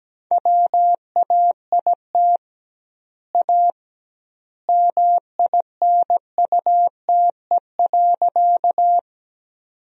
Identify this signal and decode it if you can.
Morse code, heard as a keyed tone, about 17 wpm, 700 Hz. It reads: WAIT A MINUTE.
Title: 2x Word-Spacing - Sentences from Top 500 Words - 17wpm